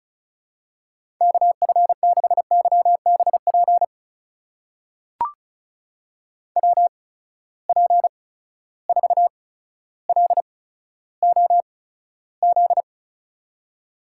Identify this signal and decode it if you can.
Morse code, heard as a keyed tone, about 36 wpm, spaced as for 15 wpm, 700 Hz. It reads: KF6YBP WP4LOZ
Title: USA Call Signs - 6 Characters - 8x Character Spacing - 35wpm